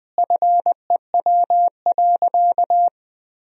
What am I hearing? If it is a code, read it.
Morse code, heard as a keyed tone, about 21 wpm, 700 Hz. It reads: FEW.
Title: Sentences from Top 1000 Words - Review Part 1 - 20wpm